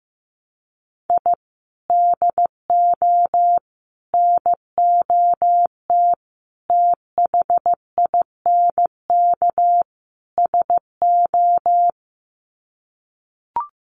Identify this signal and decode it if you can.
Morse code, heard as a keyed tone, about 15 wpm, 700 Hz. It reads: I DO NOT THINK SO
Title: Sets of 5 Words 15wpm